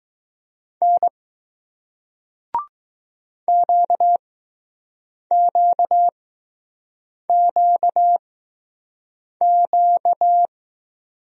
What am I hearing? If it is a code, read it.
Morse code, heard as a keyed tone, about 20 wpm, spaced as for 6 wpm, 700 Hz. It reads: N QQQQ